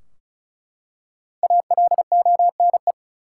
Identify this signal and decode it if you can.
Morse code, heard as a keyed tone, about 36 wpm, 700 Hz. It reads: ALONE
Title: Five Letter Words - 35wpm